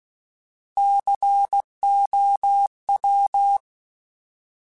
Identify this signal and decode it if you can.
Morse code, heard as a keyed tone, about 16 wpm, 800 Hz. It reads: COW
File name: Alphabetical / Alphabetical mp3s / Cs @15